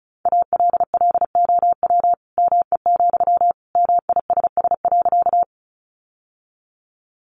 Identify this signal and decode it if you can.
Morse code, heard as a keyed tone, about 34 wpm, 700 Hz. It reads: ALLOW ME, MISS.
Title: Sentences from Top 700 Words 35wpm